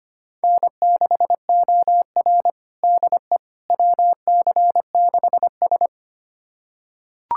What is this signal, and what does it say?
Morse code, heard as a keyed tone, about 25 wpm, 700 Hz. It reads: N6OR DE WC6S
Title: USA Callsign DE Callsign - 4 Characters - 25wpm